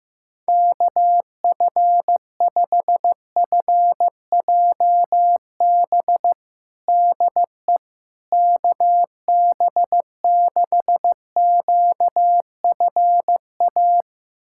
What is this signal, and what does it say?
Morse code, heard as a keyed tone, about 15 wpm, 700 Hz. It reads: KF5FJB DE KB6QFA
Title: USA Callsign DE Callsign - 6 Characters - 15wpm